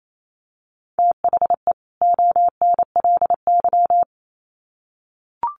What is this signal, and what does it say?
Morse code, heard as a keyed tone, about 28 wpm, 700 Hz. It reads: THE ONLY
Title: Sets of 2 Words 28wpm